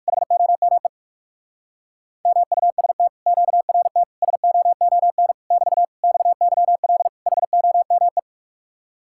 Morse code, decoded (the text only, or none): HOME MUST QRT SOON <BT> XYL HOME